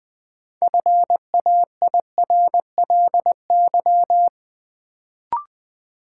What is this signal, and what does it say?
Morse code, heard as a keyed tone, about 21 wpm, 700 Hz. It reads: FAIRLY
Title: Sentences from Top 1100 Words - New Word Drill - 20wpm